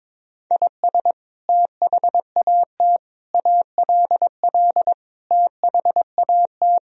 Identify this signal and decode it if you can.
Morse code, heard as a keyed tone, about 22 wpm, 700 Hz. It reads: IS THAT ALL THAT
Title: Sentences from Top 1000 Words - Review Part 3 - 22wpm